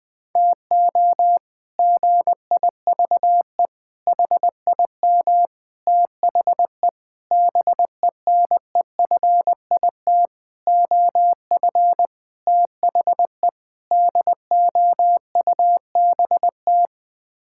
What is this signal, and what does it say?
Morse code, heard as a keyed tone, about 20 wpm, 700 Hz. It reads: TO GIVE HIM THE BENEFIT OF THE DOUBT
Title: Sets of 8 Words - 20wpm